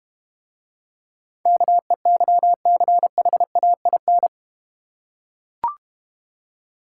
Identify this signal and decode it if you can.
Morse code, heard as a keyed tone, about 32 wpm, 700 Hz. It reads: KEYCHAIN